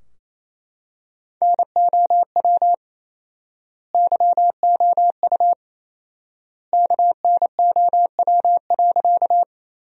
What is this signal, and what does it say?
Morse code, heard as a keyed tone, about 28 wpm, 700 Hz. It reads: NOW YOU KNOW.
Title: 4x Word-Spacing - Sentences from Top 100 Words - 28wpm